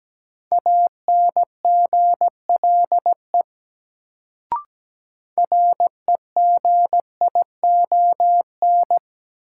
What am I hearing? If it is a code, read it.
Morse code, heard as a keyed tone, about 17 wpm, 700 Hz. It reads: ANGLE REGION
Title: Sentences from Top 1500 Words - New Word Drill Alt - 17wpm